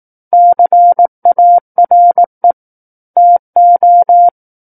Morse code, heard as a keyed tone, about 18 wpm, 700 Hz. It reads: CARE TO
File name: 18 WPM